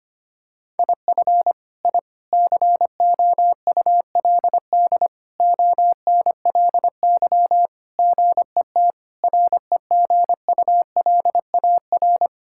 Morse code, heard as a keyed tone, about 25 wpm, 700 Hz. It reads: IF I COULD ONLY GET REGULAR
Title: Sentences from Top 1000 Words - Review Part 6 - 25wpm